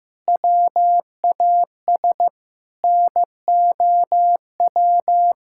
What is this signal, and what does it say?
Morse code, heard as a keyed tone, about 15 wpm, 700 Hz. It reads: WAS NOW
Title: Sentences from Top 1000 Words - Review Part 6 - 15wpm